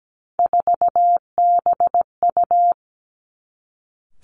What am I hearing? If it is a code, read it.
Morse code, heard as a keyed tone, about 17 wpm, 700 Hz. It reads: 4BU
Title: Three Challenging Letter-Number Combinations 17wpm